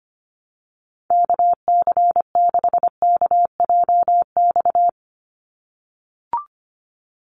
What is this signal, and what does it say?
Morse code, heard as a keyed tone, about 25 wpm, 690 Hz. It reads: KC6KJX
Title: USA Call Signs 25wpm